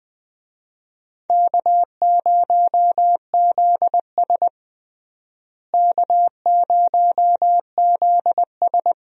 Morse code, heard as a keyed tone, about 20 wpm, 700 Hz. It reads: K0ZS K0ZS
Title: USA Call Signs - 4 Characters - Repeated 2x - 20wpm